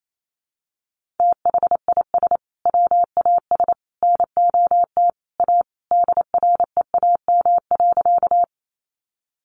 Morse code, heard as a keyed tone, about 28 wpm, 700 Hz. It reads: THIS WAS NOT A DREAM.
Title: Sentences from Top 1500 Words 28wpm